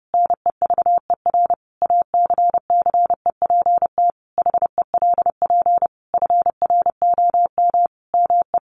Morse code, heard as a keyed tone, about 28 wpm, 700 Hz. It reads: NEVER ACCEPT HELP FROM ME
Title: Sentences from Top 1300 Words 30wpm